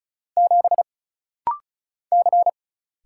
Morse code, heard as a keyed tone, about 36 wpm, 700 Hz. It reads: 7 C
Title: Single Letter-Number - Rapid-Fire 35wpm